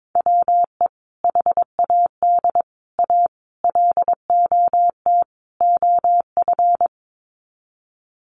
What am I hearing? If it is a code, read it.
Morse code, heard as a keyed tone, about 22 wpm, 700 Hz. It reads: WE HAD A LOT OF
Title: Sets of 5 Words Encore 22wpm